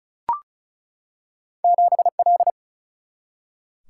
Morse code, heard as a keyed tone, about 36 wpm, 700 Hz. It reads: 7L
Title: Single-Letter Single-Number 35wpm